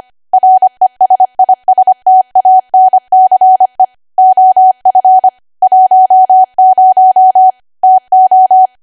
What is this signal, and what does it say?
Morse code, heard as a keyed tone, about 25 wpm, 750 Hz. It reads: RESISTANCE OF 10 TO